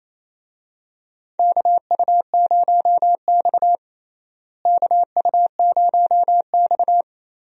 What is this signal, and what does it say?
Morse code, heard as a keyed tone, about 28 wpm, 700 Hz. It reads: KU0X KU0X
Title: USA Call Signs - 4 Characters - Repeated 2x - 28wpm